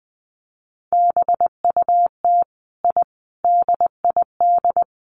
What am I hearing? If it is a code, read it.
Morse code, heard as a keyed tone, about 20 wpm, 700 Hz. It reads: BUT I DID
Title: Sets of 4 Words 20wpm